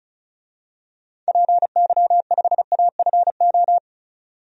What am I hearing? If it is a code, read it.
Morse code, heard as a keyed tone, about 36 wpm, 700 Hz. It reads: PY5AFO
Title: DXCC-100 Call Signs - 6 Characters - 35wpm